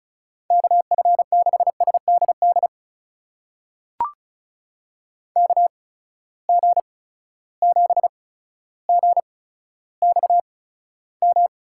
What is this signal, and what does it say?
Morse code, heard as a keyed tone, about 36 wpm, spaced as for 15 wpm, 700 Hz. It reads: KF6SDD KG7GXM
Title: USA Call Signs - 6 Characters - 8x Character Spacing - 35wpm